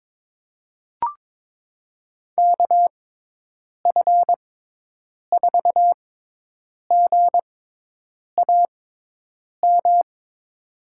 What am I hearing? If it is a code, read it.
Morse code, heard as a keyed tone, about 22 wpm, spaced as for 8 wpm, 700 Hz. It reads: KF4GAM
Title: USA Call Signs - 6 Characters - 6x Character Spacing - 22wpm